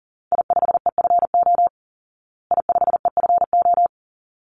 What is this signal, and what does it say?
Morse code, heard as a keyed tone, about 40 wpm, 700 Hz. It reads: I5EFO I5EFO
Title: Call Signs Repeated 3x - 40wpm